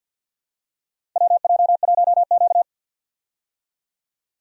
Morse code, heard as a keyed tone, about 53 wpm, 700 Hz. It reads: WJ1Q
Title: USA Call Signs - 4 Characters - 50wpm